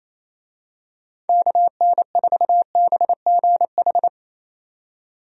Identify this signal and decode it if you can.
Morse code, heard as a keyed tone, about 28 wpm, 700 Hz. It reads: KN4BGH
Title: USA Call Signs - 6 Characters - 28wpm